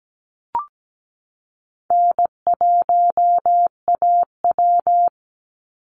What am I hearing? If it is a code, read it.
Morse code, heard as a keyed tone, about 17 wpm, 700 Hz. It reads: N1AW
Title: NA SOTA Chaser Callsigns 17wpm